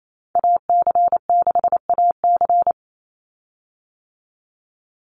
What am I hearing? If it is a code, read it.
Morse code, heard as a keyed tone, about 28 wpm, 700 Hz. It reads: AC6AC
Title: Call Signs - 28wpm